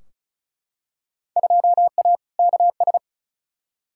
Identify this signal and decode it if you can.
Morse code, heard as a keyed tone, about 34 wpm, 700 Hz. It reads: 2A KS
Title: Field Day Exchanges - 35wpm